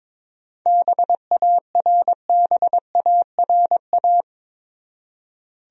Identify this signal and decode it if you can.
Morse code, heard as a keyed tone, about 22 wpm, 700 Hz. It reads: BARBARA
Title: USA Common Names - 22wpm